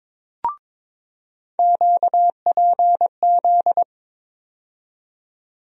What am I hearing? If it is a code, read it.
Morse code, heard as a keyed tone, about 22 wpm, 700 Hz. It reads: QPZ
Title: USA Call Sign Postfixes - 3 Letters 22wpm